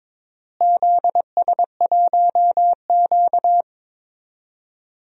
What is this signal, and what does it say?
Morse code, heard as a keyed tone, about 22 wpm, 700 Hz. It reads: ZS1Q
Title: DXCC-100 Call Signs - 4 Characters - Repeated 3x - 22wpm